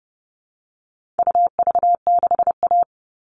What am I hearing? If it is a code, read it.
Morse code, heard as a keyed tone, about 30 wpm, 700 Hz. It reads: UV6A
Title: All Cut Numbers - 4-Digit 30wpm